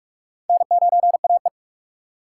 Morse code, heard as a keyed tone, about 46 wpm, 700 Hz. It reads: N9AE
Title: USA Call Signs - 2x Character Spacing - 45wpm